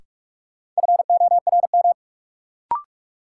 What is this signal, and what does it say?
Morse code, heard as a keyed tone, about 42 wpm, 700 Hz. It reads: FORM
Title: Sentences from Top 1000 Words - Review Part 1 - New Word Drill - 45wpm